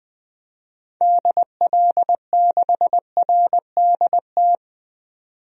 Morse code, heard as a keyed tone, about 20 wpm, 700 Hz. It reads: DL6RDT
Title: DXCC-100 Call Signs - 6 Characters - 20wpm